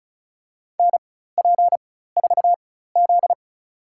Morse code, heard as a keyed tone, about 36 wpm, spaced as for 16 wpm, 700 Hz. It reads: NP4Z
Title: USA Call Signs - 4 Characters - 4x Character Spacing - 35wpm